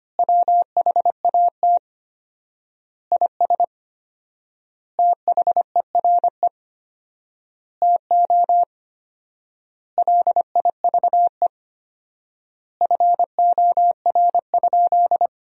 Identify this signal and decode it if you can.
Morse code, heard as a keyed tone, about 25 wpm, 700 Hz. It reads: WHAT IS THERE TO LIVE FOR?
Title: 4x Word-Spacing - Sentences from Top 200 Words - 25wpm